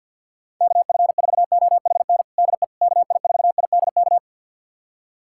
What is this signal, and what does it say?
Morse code, heard as a keyed tone, about 51 wpm, 700 Hz. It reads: KF4OHN DE KI4IDK
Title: USA Callsign DE Callsign - 6 Characters - 50wpm